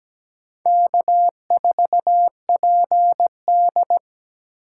Morse code, heard as a keyed tone, about 17 wpm, 700 Hz. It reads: K4PD
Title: USA Call Signs - 6x Character Spacing - 17wpm